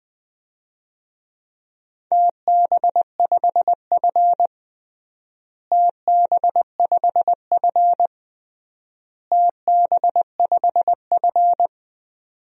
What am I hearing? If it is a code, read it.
Morse code, heard as a keyed tone, about 20 wpm, 700 Hz. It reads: TB5F TB5F TB5F
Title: DXCC-100 Call Signs - 4 Characters - Repeated 3x - 20wpm